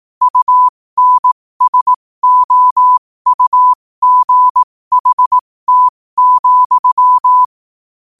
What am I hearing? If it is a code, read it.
Morse code, heard as a keyed tone, about 18 wpm, spaced as for 15 wpm, 1000 Hz. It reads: UNSOUGHT,
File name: qotdfiles / 15wpm